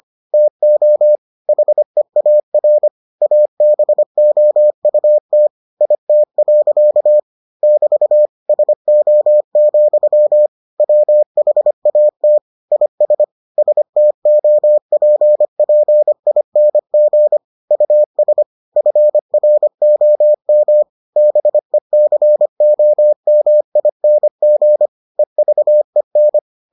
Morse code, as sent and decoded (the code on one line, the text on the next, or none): - --- / .... .- .-. / .- -... --- ..- - / .. - .-.-.- / -...- / ... --- --..-- / .-- .... .- - / .. ... / ... - --- .--. .--. .. -. --. / ..- ... / ..-. .-. --- -- / -... -.-. --- -- .. -. --. / . ...- . -.
TO HEAR ABOUT IT. <BT> SO, WHAT IS STOPPING US FROM BECOMING EVEN